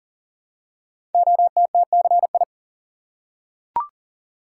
Morse code, heard as a keyed tone, about 39 wpm, 700 Hz. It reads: OTTCI
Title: Code Groups - 5 Letters - 40wpm